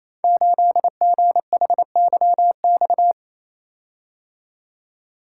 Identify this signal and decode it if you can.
Morse code, heard as a keyed tone, about 28 wpm, 700 Hz. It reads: 8GHYX
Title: Code Groups - 5 Letter:Numbers - 28wpm